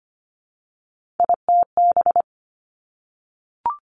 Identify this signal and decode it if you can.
Morse code, heard as a keyed tone, about 25 wpm, 700 Hz. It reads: ITB
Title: Three Challenging Letter-Number Combinations 25wpm